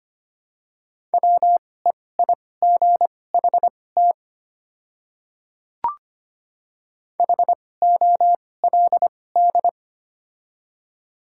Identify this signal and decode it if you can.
Morse code, heard as a keyed tone, about 25 wpm, spaced as for 18 wpm, 700 Hz. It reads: WEIGHT HOLD
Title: 2x Character Spacing - Top 500 Words - 25wpm